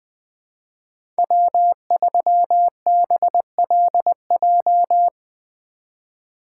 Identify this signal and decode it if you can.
Morse code, heard as a keyed tone, about 20 wpm, 700 Hz. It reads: W3BLJ
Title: USA Call Signs - 5 Characters - 3x Character Spacing - 20wpm